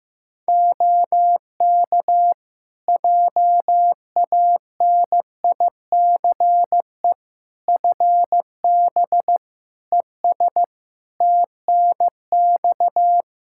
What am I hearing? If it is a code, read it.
Morse code, heard as a keyed tone, about 15 wpm, 700 Hz. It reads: OK JANICE FB ES TNX
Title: K1USN SST QSO Practice 15wpm